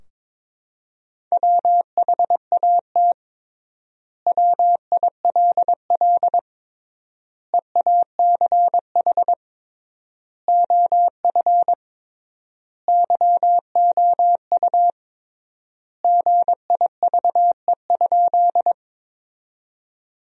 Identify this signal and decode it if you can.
Morse code, heard as a keyed tone, about 22 wpm, 700 Hz. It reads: WHAT WILL EACH OF YOU GIVE?
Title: 3x Word-Spacing - Sentences from Top 200 Words - 22wpm